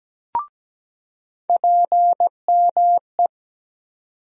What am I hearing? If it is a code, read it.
Morse code, heard as a keyed tone, about 17 wpm, 700 Hz. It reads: PME
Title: USA Call Sign Postfixes - 3 Letters 17wpm